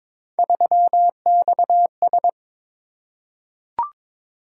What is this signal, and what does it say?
Morse code, heard as a keyed tone, about 22 wpm, 700 Hz. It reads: 3XS
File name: practice / 22wpm / Three Letter-Number Combinations - 22wpm